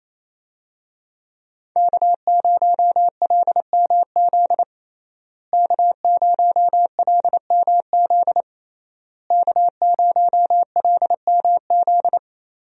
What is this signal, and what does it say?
Morse code, heard as a keyed tone, about 28 wpm, 700 Hz. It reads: K0LMZ K0LMZ K0LMZ
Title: USA Call Signs - 5 Characters - Repeated 3x - 28wpm